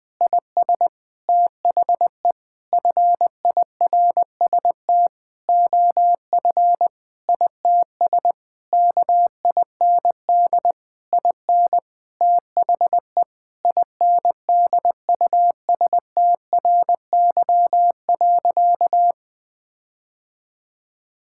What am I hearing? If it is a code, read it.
Morse code, heard as a keyed tone, about 20 wpm, 700 Hz. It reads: IS THE FIRST OF ITS KIND IN THE INDUSTRY.
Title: Sentences with 8-Word Phrases - Spoken Beginning - 20wpm